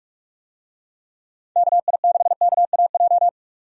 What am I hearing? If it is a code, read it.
Morse code, heard as a keyed tone, about 43 wpm, 700 Hz. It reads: KI6KAJ
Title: USA Call Signs - 6 Characters - 45wpm